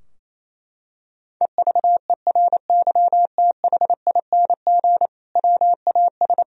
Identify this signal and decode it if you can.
Morse code, heard as a keyed tone, about 27 wpm, 700 Hz. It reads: EVERYTHING WAS
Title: Sentences from Top 1000 Words - Review Part 2 - 28wpm